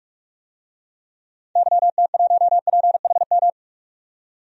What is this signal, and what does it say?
Morse code, heard as a keyed tone, about 46 wpm, 700 Hz. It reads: YT1PHM
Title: DXCC-100 Call Signs - 6 Characters - 45wpm